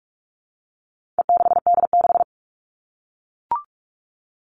Morse code, heard as a keyed tone, about 44 wpm, 700 Hz. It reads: E6D6
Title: All Cut Numbers - 4-Digit 45wpm